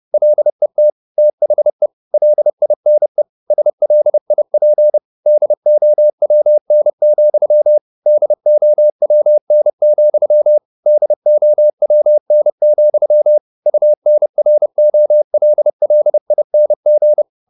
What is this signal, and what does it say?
Morse code, heard as a keyed tone, about 30 wpm, 600 Hz. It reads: LET THE LINE SLIP DOWN, DOWN, DOWN, UNROLLING
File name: the old man and the sea / the old man and the sea 1 30wpm